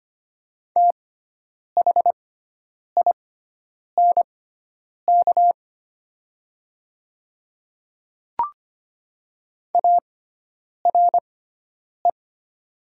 Morse code, heard as a keyed tone, about 25 wpm, spaced as for 9 wpm, 700 Hz. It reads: THINK ARE
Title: 6x Character Spacing - Top 200 Words - 25wpm